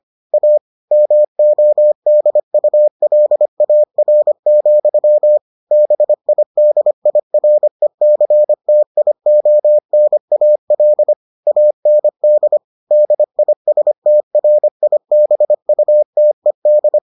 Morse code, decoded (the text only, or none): A MODULAR, BIDIRECTIONAL AND DISTRIBUTED